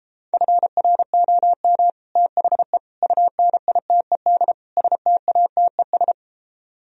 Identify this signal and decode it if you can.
Morse code, heard as a keyed tone, about 33 wpm, 700 Hz. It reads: FROM THE UNITED STATES